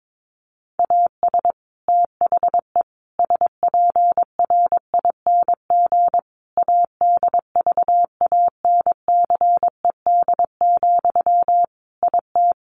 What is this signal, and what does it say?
Morse code, heard as a keyed tone, about 22 wpm, 700 Hz. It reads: AS THE SPRING ADVANCED, IT
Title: Sentences from Top 1200 Words 22wpm